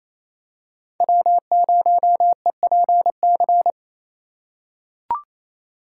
Morse code, heard as a keyed tone, about 28 wpm, 700 Hz. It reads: W0EPC
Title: K1USN SST QSO Practice - 28wpm